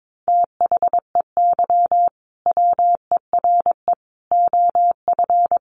Morse code, heard as a keyed tone, about 22 wpm, 700 Hz. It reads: THEY WERE OF
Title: Sentences from Top 1200 Words 22wpm